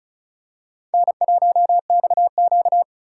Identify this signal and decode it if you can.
Morse code, heard as a keyed tone, about 36 wpm, 700 Hz. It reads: N1XQ
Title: USA Call Signs - 4 Characters - 4x Character Spacing - 35wpm